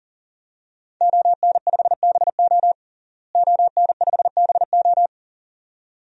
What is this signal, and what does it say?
Morse code, heard as a keyed tone, about 40 wpm, 700 Hz. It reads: ON5BO ON5BO
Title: DXCC-100 Call Signs - 5 Characters - Repeated 2x - 40wpm